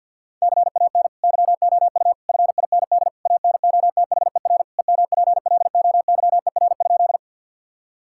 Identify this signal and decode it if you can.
Morse code, heard as a keyed tone, about 50 wpm, 700 Hz. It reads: CAN YOU FIND ANOTHER EMPLOYER?
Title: Sentences from Top 1600 Words 50wpm